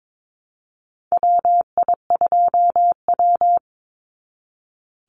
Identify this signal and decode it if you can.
Morse code, heard as a keyed tone, about 22 wpm, 700 Hz. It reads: WI2W
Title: NA SOTA Chaser Callsigns 22wpm